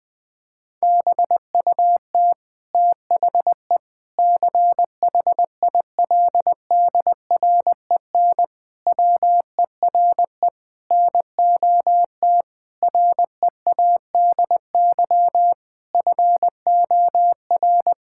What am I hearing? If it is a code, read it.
Morse code, heard as a keyed tone, about 21 wpm, 700 Hz. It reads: BUT THE CHILDREN WERE NOT READY FOR
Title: Sentences from Top 1000 Words - Review Part 6 - 20wpm